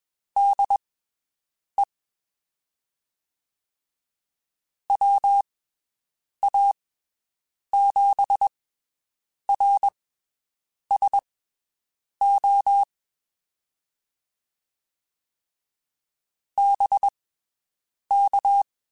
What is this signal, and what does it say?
Morse code, heard as a keyed tone, about 21 wpm, spaced as for 7 wpm, 800 Hz. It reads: DE WA7RSO BK
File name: (spaced) / QSOs Spaced-mp3s / WD0BGP 1 @20 sp